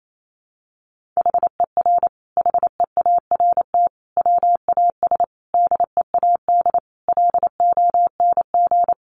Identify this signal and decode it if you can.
Morse code, heard as a keyed tone, about 28 wpm, 700 Hz. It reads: HER HEART WAS DEAD LONG
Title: Sentences from Top 1500 Words 28wpm